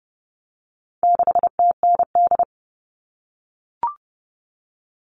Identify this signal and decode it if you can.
Morse code, heard as a keyed tone, about 30 wpm, 700 Hz. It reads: 6TND
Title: All Cut Numbers - 4-Digit 30wpm